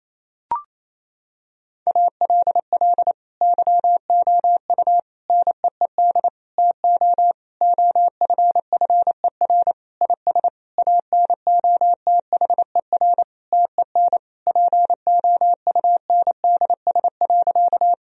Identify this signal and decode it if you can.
Morse code, heard as a keyed tone, about 27 wpm, 700 Hz. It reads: ALL YOU NEED TO OFFER IS ANOTHER TEN POUNDS.
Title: Sentences from Top 1000 Words - Review Part 4 - 28wpm